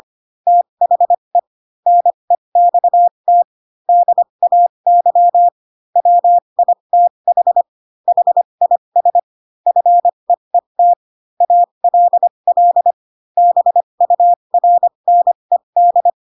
Morse code, heard as a keyed tone, about 25 wpm, spaced as for 22 wpm, 700 Hz. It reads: THE NEXT DAY WITH HIS FEET ALL BURNED
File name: Pinocchio / Pinocchio ch6 18 25 ik1hge